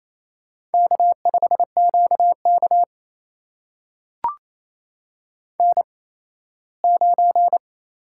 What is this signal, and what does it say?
Morse code, heard as a keyed tone, about 28 wpm, spaced as for 17 wpm, 700 Hz. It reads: K5QK N9
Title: USA Call Signs - 8x Character Spacing - 28wpm